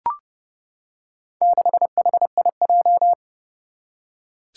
Morse code, heard as a keyed tone, about 30 wpm, 700 Hz. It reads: 6HIJ
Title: Four Challenging Letter-Number Combinations 30wpm